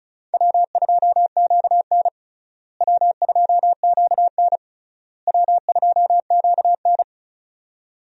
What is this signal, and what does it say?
Morse code, heard as a keyed tone, about 36 wpm, 700 Hz. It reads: W2QN W2QN W2QN
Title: USA Call Signs - 4 Characters - Repeated 3x - 35wpm